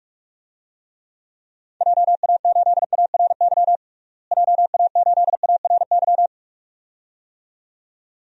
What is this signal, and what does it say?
Morse code, heard as a keyed tone, about 46 wpm, 700 Hz. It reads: JA8ARY JA8ARY
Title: DXCC-100 Call Signs - 6 Characters - Repeated 2x - 45wpm